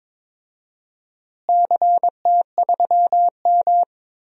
Morse code, heard as a keyed tone, about 21 wpm, 700 Hz. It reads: CT3M
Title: DXCC-100 Call Signs - 4 Characters - Repeated 2x - 22wpm